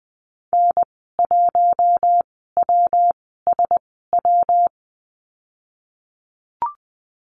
Morse code, heard as a keyed tone, about 20 wpm, spaced as for 15 wpm, 700 Hz. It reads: N1WSW
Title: 122 Lesson 20g - Number 3 - Call Signs 20wpm